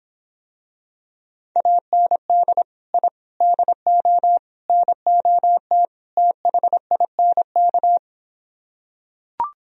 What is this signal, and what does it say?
Morse code, heard as a keyed tone, about 26 wpm, 700 Hz. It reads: AND I DO NOT THINK